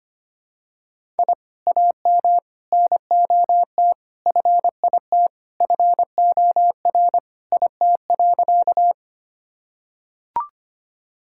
Morse code, heard as a keyed tone, about 25 wpm, 700 Hz. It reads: I AM NOT FIT FOR IT.
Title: Sentences from Top 1000 Words - Review Part 5 - 25wpm